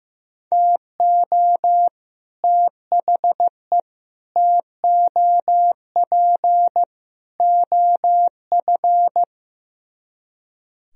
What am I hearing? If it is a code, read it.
Morse code, heard as a keyed tone, about 15 wpm, 700 Hz. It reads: TO THE TOP OF
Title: Speed-Racing - Sets of 4 Words 15wpm